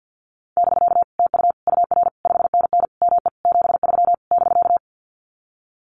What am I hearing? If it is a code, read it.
Morse code, heard as a keyed tone, about 50 wpm, 700 Hz. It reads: <BK> TU UR 5NN ME 73 <BK>
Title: Common QSO Phrases Encore 50wpm